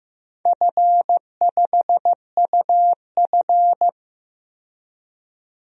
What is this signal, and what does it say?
Morse code, heard as a keyed tone, about 16 wpm, 700 Hz. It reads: F5UF
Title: DXCC-100 Call Signs - 4 Characters - 15wpm